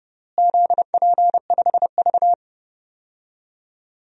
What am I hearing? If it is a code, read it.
Morse code, heard as a keyed tone, about 31 wpm, 700 Hz. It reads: ZP5V
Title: DXCC-100 Call Signs - 4 Characters - Repeated 3x - 30wpm